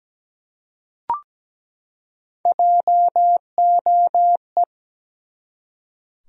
Morse code, heard as a keyed tone, about 17 wpm, 700 Hz. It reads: JOE
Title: USA Common Names - Max 5 Letters - 17wpm